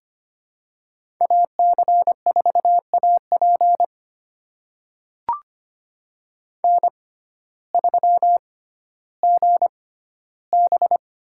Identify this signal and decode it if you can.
Morse code, heard as a keyed tone, about 25 wpm, spaced as for 14 wpm, 700 Hz. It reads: AC4AP N3GB
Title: USA Call Signs - 6x Character Spacing - 25wpm